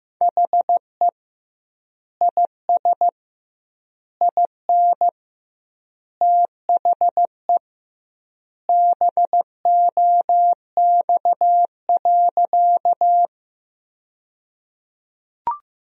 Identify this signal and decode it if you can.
Morse code, heard as a keyed tone, about 16 wpm, 700 Hz. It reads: HE IS IN THE BOX.
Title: 2x Word-Spacing - Sentences from Top 500 Words - 15wpm